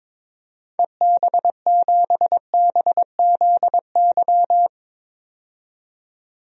Morse code, heard as a keyed tone, about 22 wpm, 700 Hz. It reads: EB7BZY
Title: DXCC-100 Call Signs - 6 Characters - 22wpm